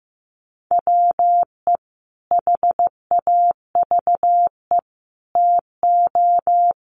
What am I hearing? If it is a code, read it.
Morse code, heard as a keyed tone, about 15 wpm, 700 Hz. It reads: WE HAVE TO
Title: A Sound Of Thunder 15wpm